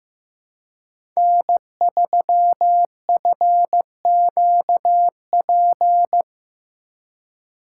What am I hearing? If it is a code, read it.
Morse code, heard as a keyed tone, about 15 wpm, 700 Hz. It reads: N3FQP
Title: USA Call Signs - 5 Characters - 3x Character Spacing - 15wpm